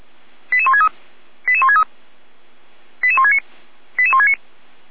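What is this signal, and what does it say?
Five-tone selective call (ZVEI), heard as quick runs of five tones, tones of about 70 ms. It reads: 88163 88163 88169 88169
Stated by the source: ZVEI five-tone selective call received by SDR, logged by Feuerwehr Orlamünde.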